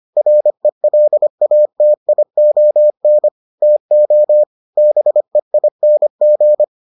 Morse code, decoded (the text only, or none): RELATION TO BEING